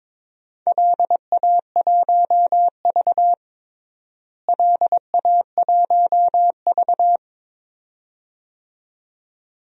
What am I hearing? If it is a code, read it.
Morse code, heard as a keyed tone, about 22 wpm, 700 Hz. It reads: LA1V LA1V
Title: DXCC-100 Call Signs - 4 Characters - Repeated 2x - 22wpm